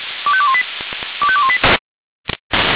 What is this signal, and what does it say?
Five-tone selective call (ZVEI), heard as quick runs of five tones, tones of about 70 ms. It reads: 25218 25218